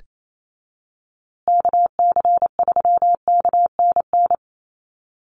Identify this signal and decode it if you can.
Morse code, heard as a keyed tone, about 28 wpm, 700 Hz. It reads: KC3KNN
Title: USA Call Signs Repeated 3x 28wpm